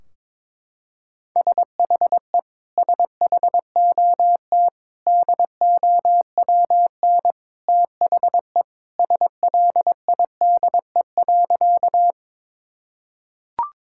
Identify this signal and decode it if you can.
Morse code, heard as a keyed tone, about 22 wpm, 700 Hz. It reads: SHE SHOT DOWN THE SLIDE.
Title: Sentences from Top 1600 Words 22wpm